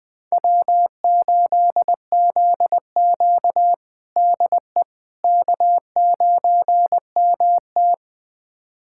Morse code, heard as a keyed tone, about 21 wpm, 700 Hz. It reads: W8ZQ DE K9MT
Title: USA Callsign DE Callsign - 4 Characters - 20wpm